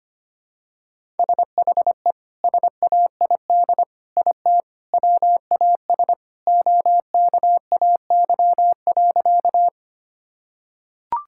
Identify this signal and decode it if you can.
Morse code, heard as a keyed tone, about 25 wpm, 700 Hz. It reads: SHE SAID IT WAS OKAY.
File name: practice / 25wpm / Sentences with 2-Word Phrases - 25wpm